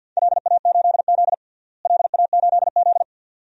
Morse code, heard as a keyed tone, about 50 wpm, 700 Hz. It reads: LA8Z LA8Z
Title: DXCC-100 Call Signs - 4 Characters - Repeated 2x - 50wpm